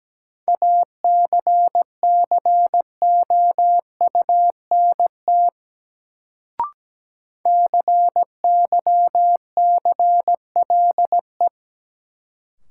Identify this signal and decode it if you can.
Morse code, heard as a keyed tone, about 17 wpm, 700 Hz. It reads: ACCOUNT CYCLE
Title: Sentences from Top 900 Words - New Word Drill - 17wpm